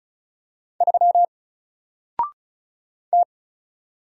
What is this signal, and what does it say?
Morse code, heard as a keyed tone, about 34 wpm, 700 Hz. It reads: 3 T
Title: Single Letter-Number 35wpm